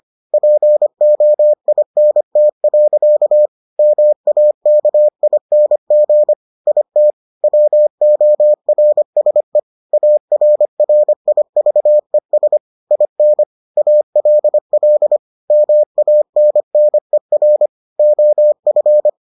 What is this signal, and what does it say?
Morse code, heard as a keyed tone, about 25 wpm, 600 Hz. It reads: POINT. MAKING IT WORSE ARRIVES IN ALL MANNER OF